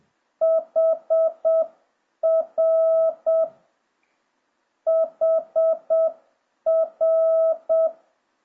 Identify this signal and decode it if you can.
Morse code, heard as a keyed tone, about 7 wpm, 650 Hz. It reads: HR HR